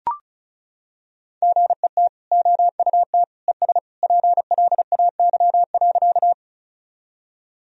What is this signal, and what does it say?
Morse code, heard as a keyed tone, about 34 wpm, 700 Hz. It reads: GET OUT ES PLAY.
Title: Sentences from Top 200 Words and Abbreviations 35wpm